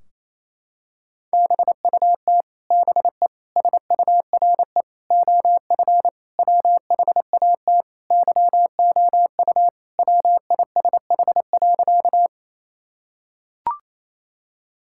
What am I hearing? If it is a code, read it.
Morse code, heard as a keyed tone, about 28 wpm, 700 Hz. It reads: BUT BE SURE OF WHAT YOU WISH.
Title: Sentences from Top 1600 Words 28wpm